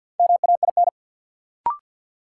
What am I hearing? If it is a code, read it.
Morse code, heard as a keyed tone, about 53 wpm, 700 Hz. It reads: MAIN
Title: Sentences from Top 1300 Words - New Word Drill Alt - 50wpm